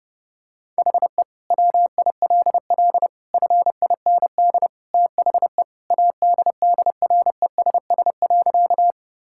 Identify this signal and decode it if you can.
Morse code, heard as a keyed tone, about 30 wpm, 700 Hz. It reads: HE WILL FIND THE ADDRESS.
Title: Sentences from Top 1000 Words - Review Part 6 - 30wpm